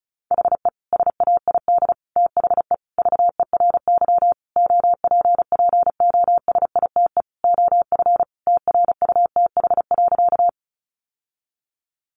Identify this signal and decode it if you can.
Morse code, heard as a keyed tone, about 34 wpm, 700 Hz. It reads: HE SAID THE VERY OPPOSITE OF TRUTH.
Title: Sentences from Top 1300 Words 35wpm